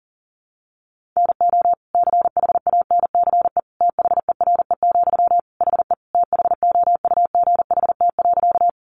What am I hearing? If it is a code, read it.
Morse code, heard as a keyed tone, about 40 wpm, 700 Hz. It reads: NO CHANCE THERE, HE THOUGHT.
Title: Sentences from Top 800 Words 40wpm